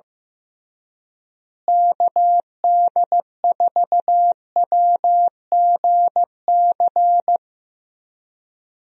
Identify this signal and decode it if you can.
Morse code, heard as a keyed tone, about 16 wpm, 700 Hz. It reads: KD4WGC
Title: USA Call Signs - 6 Characters - Repeated 3x - 15wpm